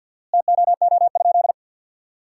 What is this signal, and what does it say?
Morse code, heard as a keyed tone, about 51 wpm, 700 Hz. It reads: TOO?
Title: 4x Word-Spacing - Sentences from Top 500 Words - 50wpm